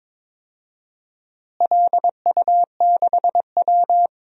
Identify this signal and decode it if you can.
Morse code, heard as a keyed tone, about 22 wpm, 700 Hz. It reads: LU6W
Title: DXCC-100 Call Signs - 4 Characters - Repeated 2x - 22wpm